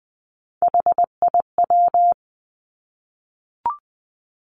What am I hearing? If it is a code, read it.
Morse code, heard as a keyed tone, about 20 wpm, 700 Hz. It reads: HIW
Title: Three Challenging Letter-Number Combinations 20wpm